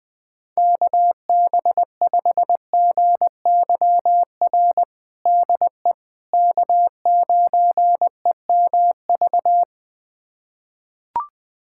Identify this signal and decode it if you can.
Morse code, heard as a keyed tone, about 19 wpm, 700 Hz. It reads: KB5GYR DE K9EMV
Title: USA Callsign DE Callsign 20wpm